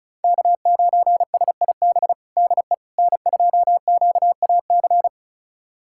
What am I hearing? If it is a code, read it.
Morse code, heard as a keyed tone, about 36 wpm, 700 Hz. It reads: K9SIB DE N2QAC
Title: USA Callsign DE Callsign - 5 Characters - 35wpm